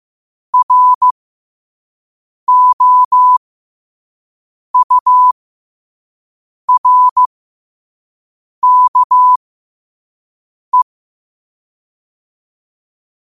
Morse code, heard as a keyed tone, about 15 wpm, spaced as for 5 wpm, 1000 Hz. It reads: ROURKE
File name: qotdfiles / 5wpm